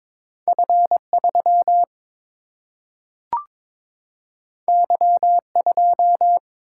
Morse code, heard as a keyed tone, about 22 wpm, 700 Hz. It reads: F3 Y2
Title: Single-Letter Single-Number 22wpm